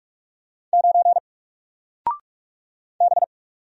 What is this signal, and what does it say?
Morse code, heard as a keyed tone, about 45 wpm, 700 Hz. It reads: 9 B
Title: Single Letter-Number 45wpm